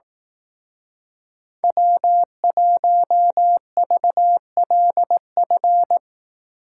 Morse code, heard as a keyed tone, about 18 wpm, 700 Hz. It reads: W1VLF
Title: Speed Racing - USA Call Signs 15wpm